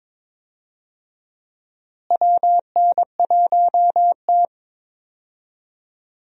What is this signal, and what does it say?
Morse code, heard as a keyed tone, about 22 wpm, 700 Hz. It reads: WN1T
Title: USA Call Signs - 4 Characters - Repeated 3x - 22wpm